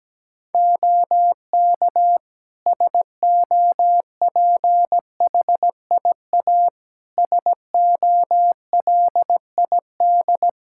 Morse code, heard as a keyed tone, about 17 wpm, 700 Hz. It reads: OK SOPHIA SOLID
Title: Practice QSO - Exchange 3 - More Info - 17wpm